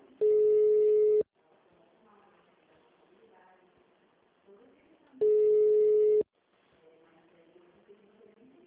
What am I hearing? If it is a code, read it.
Ringback tone (Europe, 425 Hz), with speech.
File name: out- -2012- - - .2